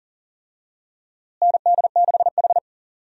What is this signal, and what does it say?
Morse code, heard as a keyed tone, about 37 wpm, 700 Hz. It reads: ND6H
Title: USA Call Signs - 4 Characters - Repeated 2x - 40wpm